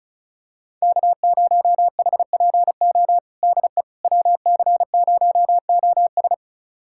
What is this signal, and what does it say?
Morse code, heard as a keyed tone, about 36 wpm, 700 Hz. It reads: K0HPO DE WC0OS
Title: USA Callsign DE Callsign - 5 Characters - 35wpm